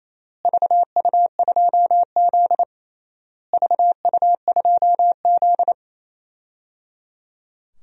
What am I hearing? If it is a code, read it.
Morse code, heard as a keyed tone, about 28 wpm, 700 Hz. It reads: VU2Z VU2Z
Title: DXCC-100 Call Signs - 4 Characters - Repeated 3x - 28wpm